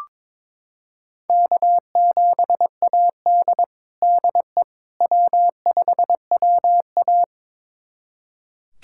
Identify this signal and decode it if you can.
Morse code, heard as a keyed tone, about 22 wpm, 700 Hz. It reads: K7AD DE W5WA
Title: USA Callsign DE Callsign - 4 Characters - 22wpm